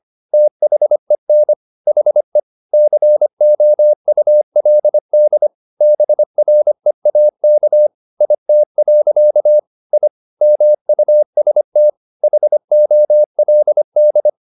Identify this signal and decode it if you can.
Morse code, heard as a keyed tone, about 25 wpm, 600 Hz. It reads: THEN HE COULD BREAK IT. I MUST HOLD
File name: the old man and the sea part 1 25wpm